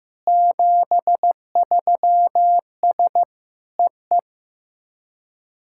Morse code, heard as a keyed tone, about 16 wpm, 700 Hz. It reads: 73S EE
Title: SOTA QSO Practice 15wpm